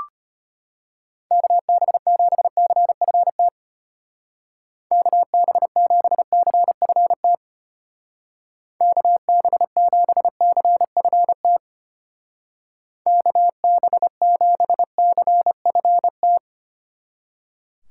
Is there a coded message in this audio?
Yes: Morse code from 1.3 to 16.4 s, heard as a keyed tone, about 32 wpm, 700 Hz. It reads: KB7CFT KB7CFT KB7CFT KB7CFT